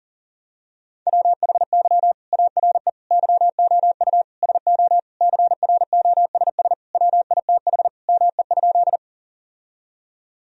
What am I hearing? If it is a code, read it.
Morse code, heard as a keyed tone, about 40 wpm, 700 Hz. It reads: WHY ARE YOU SO CROSS WITH ME?
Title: Sentences from Top 1000 Words - Review Part 1 - 40wpm